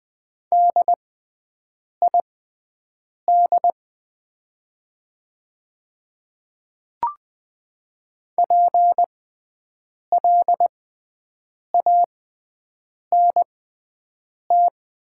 Morse code, heard as a keyed tone, about 20 wpm, spaced as for 7 wpm, 700 Hz. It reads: DID PLANT
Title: 6x Character Spacing - Top 300 Words - 20wpm